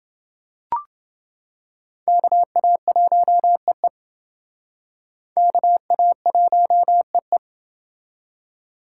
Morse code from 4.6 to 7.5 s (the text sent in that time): KA1EE